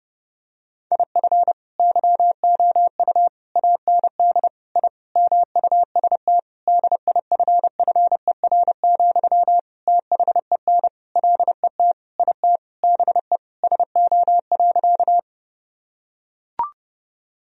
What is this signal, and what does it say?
Morse code, heard as a keyed tone, about 30 wpm, 700 Hz. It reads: IF YOU AND I MUST DIFFER, THEN LET IT BE SO.
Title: Sentences from Top 1000 Words - Review Part 1 - 30wpm